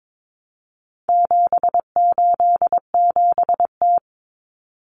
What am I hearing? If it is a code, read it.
Morse code, heard as a keyed tone, about 22 wpm, 700 Hz. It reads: 787T
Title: Common Cut Numbers - 4-Digit 22wpm